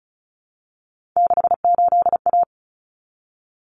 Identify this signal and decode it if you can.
Morse code, heard as a keyed tone, about 34 wpm, 700 Hz. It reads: 68A
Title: Common Cut Numbers - 3-Digit 35wpm